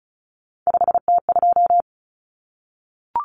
Morse code, heard as a keyed tone, about 34 wpm, 700 Hz. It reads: 5T2
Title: Common Cut Numbers - 3-Digit 35wpm